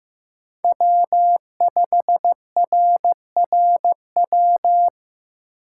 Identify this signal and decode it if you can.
Morse code, heard as a keyed tone, about 18 wpm, 700 Hz. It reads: W5RRW
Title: Speed Racing - USA Call Signs 15wpm